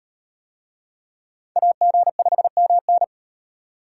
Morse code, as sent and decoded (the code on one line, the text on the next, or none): .- --. ..... -- -.
AG5MN